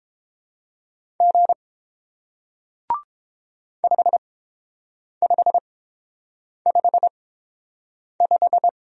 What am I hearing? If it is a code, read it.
Morse code, heard as a keyed tone, about 29 wpm, spaced as for 7 wpm, 700 Hz. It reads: G 5555